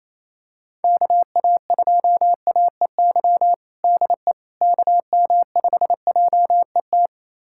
Morse code, heard as a keyed tone, about 27 wpm, 700 Hz. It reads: KA2AEY DE KM5JET
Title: USA Callsign DE Callsign - 6 Characters - 28wpm